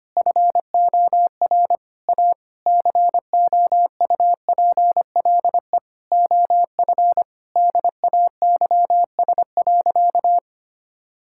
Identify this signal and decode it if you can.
Morse code, heard as a keyed tone, about 25 wpm, 700 Hz. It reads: FOR A COUPLE OF DAYS.
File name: practice / 25wpm / Sentences with 4-Word Phrases - 25wpm